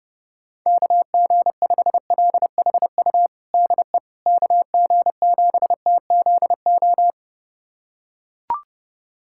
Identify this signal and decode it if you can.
Morse code, heard as a keyed tone, about 30 wpm, 700 Hz. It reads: KG5LHU DE KG7TZO
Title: USA Callsign DE Callsign - 6 Characters - 30wpm